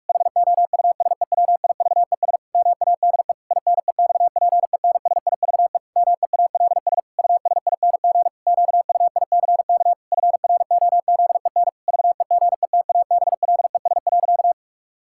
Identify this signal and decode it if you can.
Morse code, heard as a keyed tone, about 46 wpm, 700 Hz. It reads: HOUSEWIVES MADE INEXPENSIVE MEALS USING QUICK FROZEN VEGETABLES.
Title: Pangram Challenge v2 - 45wpm